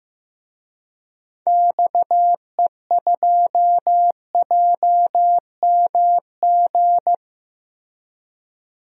Morse code, heard as a keyed tone, about 15 wpm, 700 Hz. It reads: XE2JMG
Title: DXCC-100 Call Signs - 6 Characters - Repeated 2x - 15wpm